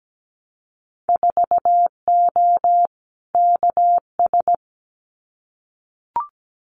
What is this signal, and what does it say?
Morse code, heard as a keyed tone, about 17 wpm, 700 Hz. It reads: 4O KS
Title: Winter Field Day Exchanges Encore 17wpm